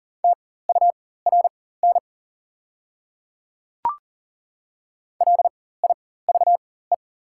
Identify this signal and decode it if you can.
Morse code, heard as a keyed tone, about 40 wpm, spaced as for 19 wpm, 690 Hz. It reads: TURN LIVE
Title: 4x Character Spacing - Top 200 Words - 40wpm